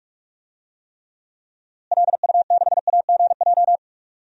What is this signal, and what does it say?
Morse code, heard as a keyed tone, about 44 wpm, 700 Hz. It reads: LU6AGJ
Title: DXCC-100 Call Signs - 6 Characters - Repeated 3x - 45wpm